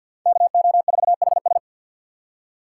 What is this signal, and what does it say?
Morse code, heard as a keyed tone, about 63 wpm, 700 Hz. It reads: KO4HS